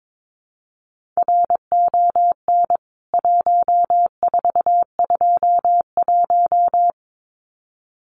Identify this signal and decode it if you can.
Morse code, heard as a keyed tone, about 22 wpm, 700 Hz. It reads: RON 1421
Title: CWT Exchange 22wpm